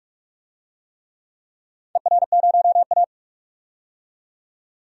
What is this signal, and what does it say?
Morse code, heard as a keyed tone, about 42 wpm, 700 Hz. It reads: ER0A
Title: DXCC-100 Call Signs - 4 Characters - 45wpm